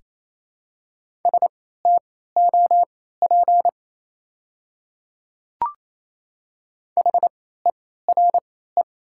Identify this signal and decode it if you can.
Morse code, heard as a keyed tone, about 28 wpm, spaced as for 16 wpm, 700 Hz. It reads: STOP HERE
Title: 3x Character Spacing - Top 300 Words - 28wpm